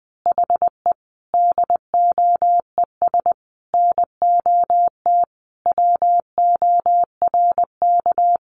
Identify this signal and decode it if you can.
Morse code, heard as a keyed tone, about 20 wpm, 700 Hz. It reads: HE DOES NOT WORK
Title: Additional Easy Phrases 20wpm